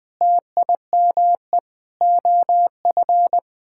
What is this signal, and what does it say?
Morse code, heard as a keyed tone, about 21 wpm, 700 Hz. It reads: TIME OF
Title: Sentences from Top 1000 Words - Review Part 6 - 20wpm